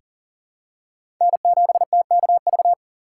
Morse code, heard as a keyed tone, about 40 wpm, 700 Hz. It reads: N7TKV
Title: USA Call Signs - 5 Characters - 6x Character Spacing - 40wpm